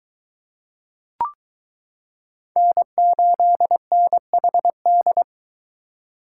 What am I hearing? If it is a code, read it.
Morse code, heard as a keyed tone, about 22 wpm, 700 Hz. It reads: N8NHD